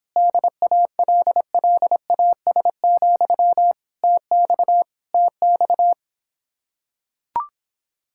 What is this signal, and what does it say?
Morse code, heard as a keyed tone, about 26 wpm, 700 Hz. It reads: DALLAS, TX TX